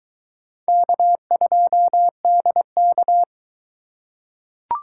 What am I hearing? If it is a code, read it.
Morse code, heard as a keyed tone, about 21 wpm, 700 Hz. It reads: K2DK